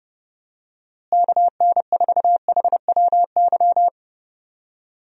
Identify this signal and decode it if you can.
Morse code, heard as a keyed tone, about 31 wpm, 700 Hz. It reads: KN4HWY
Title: USA Call Signs - 6 Characters - Repeated 3x - 30wpm